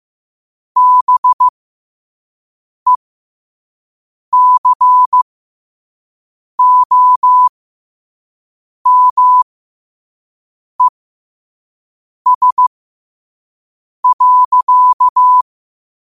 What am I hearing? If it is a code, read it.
Morse code, heard as a keyed tone, about 15 wpm, spaced as for 5 wpm, 1000 Hz. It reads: BECOMES.
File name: qotdfiles / 5wpm